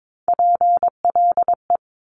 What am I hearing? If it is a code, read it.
Morse code, heard as a keyed tone, about 22 wpm, 700 Hz. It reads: PLE
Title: Top Three Letter Combinations in Words 22wpm